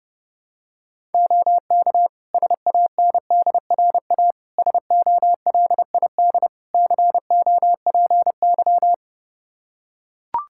Morse code, heard as a keyed tone, about 30 wpm, 700 Hz. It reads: OK SANDRA SOLID COPY
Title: Practice QSO - Exchange 3 - More Info - 30wpm